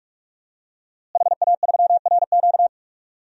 Morse code, heard as a keyed tone, about 46 wpm, 700 Hz. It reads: HA3RQ
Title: DXCC-100 Call Signs - 5 Characters - Repeated 2x - 45wpm